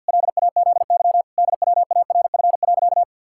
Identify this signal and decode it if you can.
Morse code, heard as a keyed tone, about 51 wpm, 700 Hz. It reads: LAZY DWARF.
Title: Pangram Challenge v2 - 50wpm